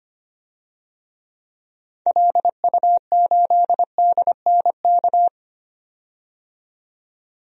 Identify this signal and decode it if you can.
Morse code, heard as a keyed tone, about 25 wpm, 700 Hz. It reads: LU8DNK
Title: DXCC-100 Call Signs - 6 Characters - 25wpm